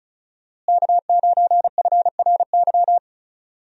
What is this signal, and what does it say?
Morse code, heard as a keyed tone, about 34 wpm, 700 Hz. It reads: K9FRY
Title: USA Call Signs - 5 Characters - Repeated 3x - 35wpm